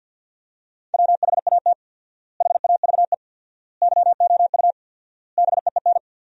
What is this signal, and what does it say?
Morse code, heard as a keyed tone, about 50 wpm, 700 Hz. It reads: WHAT HAVE YOU BEEN
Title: 4x Word-Spacing - Sentences from Top 200 Words - 50wpm